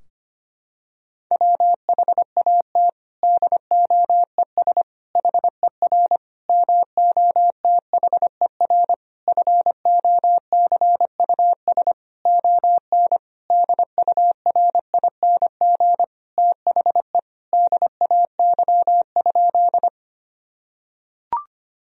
Morse code, heard as a keyed tone, about 25 wpm, 700 Hz. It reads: WHAT DOES HER MOTHER FOCUS ON DURING THE DAY?
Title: Sentences from Top 1000 Words - Review Part 4 - 25wpm